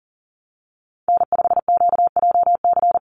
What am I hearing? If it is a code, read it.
Morse code, heard as a keyed tone, about 40 wpm, 700 Hz. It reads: N5QJC
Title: USA Call Signs 40wpm